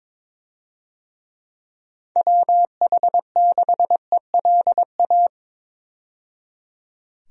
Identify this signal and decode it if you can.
Morse code, heard as a keyed tone, about 22 wpm, 700 Hz. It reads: WH6ELA
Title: USA Call Signs - 6 Characters - 22wpm